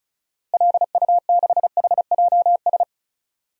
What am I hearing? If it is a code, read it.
Morse code, heard as a keyed tone, about 36 wpm, 700 Hz. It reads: LU6HJS
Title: DXCC-100 Call Signs - 6 Characters - Repeated 3x - 35wpm